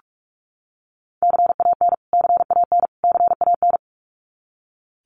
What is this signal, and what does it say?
Morse code, heard as a keyed tone, about 44 wpm, 700 Hz. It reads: CAN CAN CAN
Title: Top 500 Words Repeated 45wpm